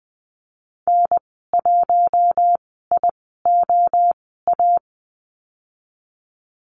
Morse code, heard as a keyed tone, about 20 wpm, spaced as for 14 wpm, 700 Hz. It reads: N1IOA
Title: 042 Lesson 8g - Number 1 - Call Signs 20wpm